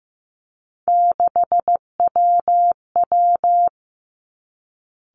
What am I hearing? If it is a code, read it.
Morse code, heard as a keyed tone, about 15 wpm, 700 Hz. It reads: 6WW
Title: Three Challenging Letter-Number Combinations 15wpm